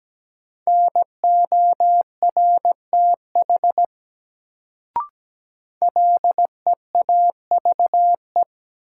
Morse code, heard as a keyed tone, about 17 wpm, 700 Hz. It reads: NORTH LEAVE
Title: Sentences from Top 400 Words - New Word Drill - 17wpm